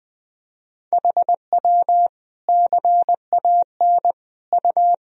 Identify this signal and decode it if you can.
Morse code, heard as a keyed tone, about 20 wpm, 700 Hz. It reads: HW CAN U
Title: Sentences from Top 100 Words and Abbreviations 20wpm